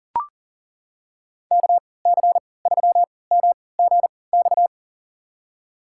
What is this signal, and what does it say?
Morse code, heard as a keyed tone, about 40 wpm, spaced as for 23 wpm, 700 Hz. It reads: KC3MGX
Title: USA Call Signs - 6 Characters - 3x Character Spacing - 40wpm